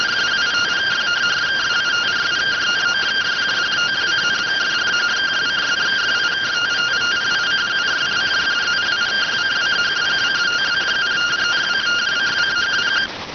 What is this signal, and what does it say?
Two-tone teleprinter signal, about 170 Hz shift.